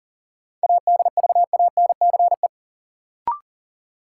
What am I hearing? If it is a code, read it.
Morse code, heard as a keyed tone, about 40 wpm, 700 Hz. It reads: ADVANCE
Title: Sentences from Top 1400 Words - New Word Drill - 40wpm